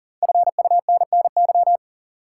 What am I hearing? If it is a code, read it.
Morse code, heard as a keyed tone, about 40 wpm, 700 Hz. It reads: FUNNY
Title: Five Letter Words - 40wpm